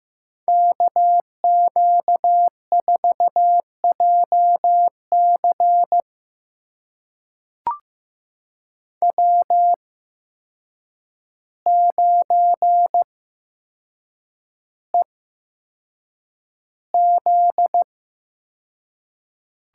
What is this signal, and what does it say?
Morse code, heard as a keyed tone, about 15 wpm, spaced as for 7 wpm, 700 Hz. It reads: KQ4JC W9EZ
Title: USA Call Signs - 5 Characters - 8x Character Spacing - 15wpm